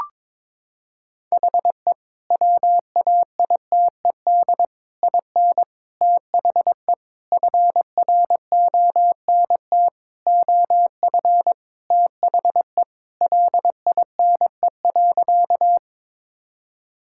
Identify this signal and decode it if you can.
Morse code, heard as a keyed tone, about 22 wpm, 700 Hz. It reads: HE WAITED IN THE FRONT OF THE LINE.
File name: practice / 22wpm / Sentences with 4-Word Phrases - 22wpm